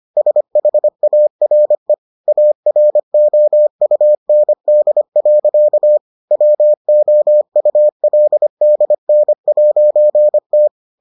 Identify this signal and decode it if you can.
Morse code, heard as a keyed tone, about 25 wpm, 600 Hz. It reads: SHARE AROUND. WOULDN'T